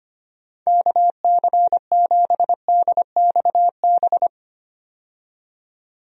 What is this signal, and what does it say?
Morse code, heard as a keyed tone, about 25 wpm, 700 Hz. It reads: KC7DXB
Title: USA Call Signs - 6 Characters - 25wpm